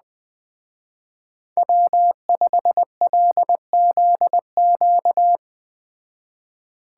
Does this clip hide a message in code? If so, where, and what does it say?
Yes: Morse code from 1.6 to 5.4 s, heard as a keyed tone, about 21 wpm, 700 Hz. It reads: W5LZQ